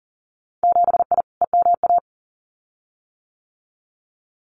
Morse code, heard as a keyed tone, about 42 wpm, 700 Hz. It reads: 7I EMA
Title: Winter Field Day Exchanges Encore 40wpm